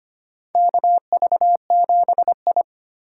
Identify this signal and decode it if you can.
Morse code, heard as a keyed tone, about 25 wpm, 700 Hz. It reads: KV7I
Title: USA Call Signs - 4 Characters - 2x Character Spacing - 25wpm